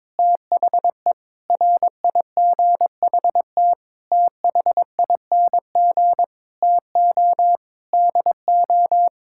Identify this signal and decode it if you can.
Morse code, heard as a keyed tone, about 22 wpm, 700 Hz. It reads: THE RIGHT THING TO DO
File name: practice / 22wpm / Sets of 8 Words - 22wpm